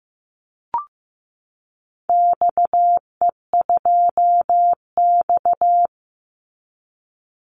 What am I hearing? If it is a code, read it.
Morse code, heard as a keyed tone, about 15 wpm, 700 Hz. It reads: XE2X
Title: Call Signs Repeated 3x - 15wpm